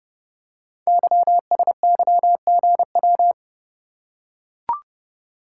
Morse code, heard as a keyed tone, about 30 wpm, 700 Hz. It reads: YSYGW
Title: Code Groups - 5 Letters - 30wpm